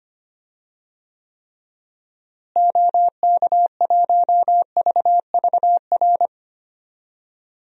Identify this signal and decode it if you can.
Morse code, heard as a keyed tone, about 25 wpm, 700 Hz. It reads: OK1VVR
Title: DXCC-100 Call Signs - 6 Characters - Repeated 3x - 25wpm